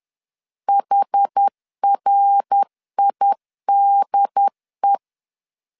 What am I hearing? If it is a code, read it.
Morse code, heard as a keyed tone, about 10 wpm, 800 Hz. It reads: HRIDE
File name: CW / DET / 1 3 1500C S B 10wpm 3335